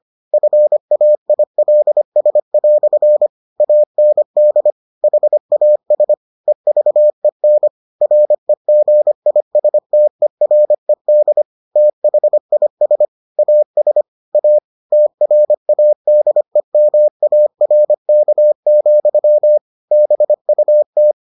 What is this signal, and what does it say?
Morse code, heard as a keyed tone, about 25 wpm, 600 Hz. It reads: FAILS" AND HAS EVEN REGISTERED THIS AS A TRADEMARK, BUT